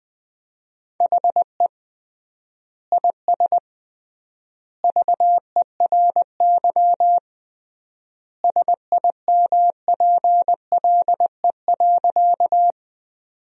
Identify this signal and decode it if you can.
Morse code, heard as a keyed tone, about 21 wpm, 700 Hz. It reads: HE IS VERY SIMPLE.
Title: 3x Word-Spacing - Sentences from Top 400 Words - 20wpm